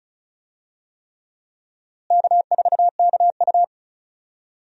Morse code, heard as a keyed tone, about 36 wpm, 700 Hz. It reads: K4KU
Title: USA Call Signs - 4 Characters - Repeated 3x - 35wpm